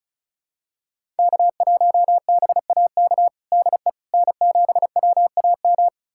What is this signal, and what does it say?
Morse code, heard as a keyed tone, about 34 wpm, 700 Hz. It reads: K1BAK DE N7WAM
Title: USA Callsign DE Callsign - 5 Characters - 35wpm